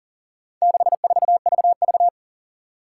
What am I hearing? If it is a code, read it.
Morse code, heard as a keyed tone, about 40 wpm, 700 Hz. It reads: 64VV
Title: Four Challenging Letter-Number Combinations 40wpm